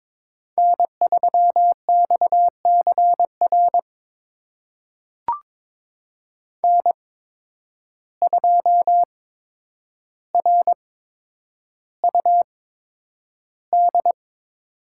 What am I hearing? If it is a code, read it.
Morse code, heard as a keyed tone, about 22 wpm, spaced as for 9 wpm, 700 Hz. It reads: N3XCR N2RUD
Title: USA Call Signs - 5 Characters - 8x Character Spacing - 22wpm